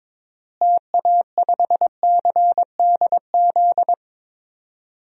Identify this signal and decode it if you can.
Morse code, heard as a keyed tone, about 22 wpm, 700 Hz. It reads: TA5CDZ
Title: DXCC-100 Call Signs - 6 Characters - Repeated 2x - 22wpm